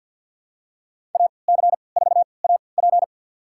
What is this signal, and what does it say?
Morse code, heard as a keyed tone, about 53 wpm, 700 Hz. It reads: A C 4 A P
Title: USA Call Signs - 5 Characters - 3x Character Spacing - 50wpm